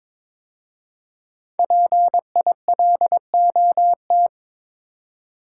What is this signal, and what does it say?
Morse code, heard as a keyed tone, about 23 wpm, 700 Hz. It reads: PILOT
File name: 5 Letter Words - First 3 Letters Given - 22wpm